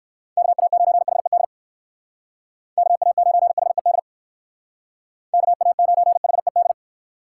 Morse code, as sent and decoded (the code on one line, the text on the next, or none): -.- .- ----. .... -.. / -.- .- ----. .... -.. / -.- .- ----. .... -..
KA9HED KA9HED KA9HED